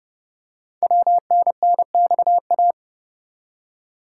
Morse code, heard as a keyed tone, about 30 wpm, 700 Hz. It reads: WNNXA
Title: Code Groups - 5 Letters - 30wpm